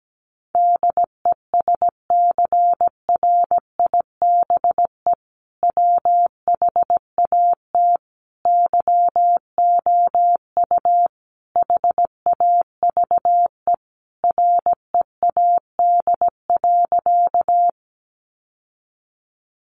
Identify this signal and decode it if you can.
Morse code, heard as a keyed tone, about 17 wpm, 700 Hz. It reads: DESCRIBE WHAT YOU HAVE READ.
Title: Sentences from Top 1000 Words 17wpm